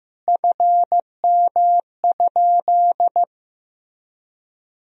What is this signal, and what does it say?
Morse code, heard as a keyed tone, about 15 wpm, 700 Hz. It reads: FM?
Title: Sentences from Top 200 Words and Abbreviations 15wpm